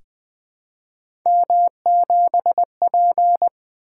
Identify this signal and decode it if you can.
Morse code, heard as a keyed tone, about 20 wpm, 700 Hz. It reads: M7P
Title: Three Letter-Number Combinations - 20wpm